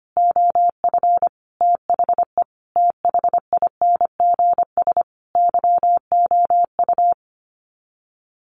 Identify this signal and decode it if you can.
Morse code, heard as a keyed tone, about 25 wpm, 700 Hz. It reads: OF THE THINGS YOU
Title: Sets of 5 Words Encore2 25wpm